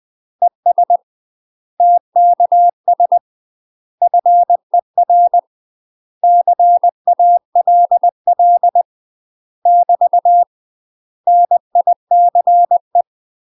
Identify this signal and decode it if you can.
Morse code, heard as a keyed tone, about 20 wpm, 700 Hz. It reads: ES TKS FER CALL <BT> NICE
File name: UK MASTER QSOs TRAINING / 20WPM / UK MASTER QSOs 0069